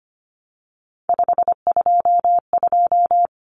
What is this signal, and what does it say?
Morse code, heard as a keyed tone, about 25 wpm, 700 Hz. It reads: 522
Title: Common Cut Numbers - 3-Digit 25wpm